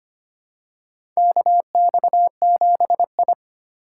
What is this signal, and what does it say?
Morse code, heard as a keyed tone, about 25 wpm, 700 Hz. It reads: KX7I
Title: USA Call Signs - 4 Characters - 4x Character Spacing - 25wpm